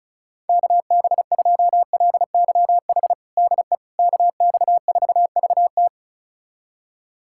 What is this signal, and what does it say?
Morse code, heard as a keyed tone, about 36 wpm, 700 Hz. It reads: KB2LYH DE KX4VT
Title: USA Callsign DE Callsign 35wpm